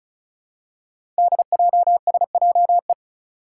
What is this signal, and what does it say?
Morse code, heard as a keyed tone, about 36 wpm, 700 Hz. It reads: DJSJE
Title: Five Challenging Letter-Number Combinations 35wpm